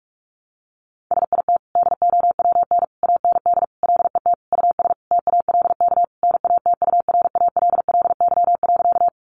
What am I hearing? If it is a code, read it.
Morse code, heard as a keyed tone, about 44 wpm, 700 Hz. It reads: SIT DOWN AND LET US TALK NATURALLY.
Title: Sentences from Top 1300 Words 45wpm